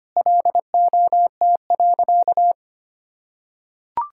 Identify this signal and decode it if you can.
Morse code, heard as a keyed tone, about 25 wpm, 700 Hz. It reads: LOT.
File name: Sentences with 3-Word Phrases - 25wpm